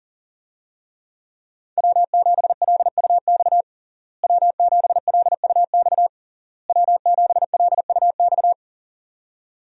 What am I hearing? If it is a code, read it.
Morse code, heard as a keyed tone, about 40 wpm, 700 Hz. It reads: W7LUX W7LUX W7LUX
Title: USA Call Signs - 5 Characters - Repeated 3x - 40wpm